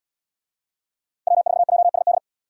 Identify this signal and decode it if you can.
Morse code, heard as a keyed tone, about 76 wpm, 700 Hz. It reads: W4JSD